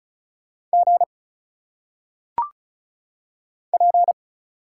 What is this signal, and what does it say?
Morse code, heard as a keyed tone, about 35 wpm, 700 Hz. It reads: G P
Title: Single Letters - Phonetic Pronunciation 35wpm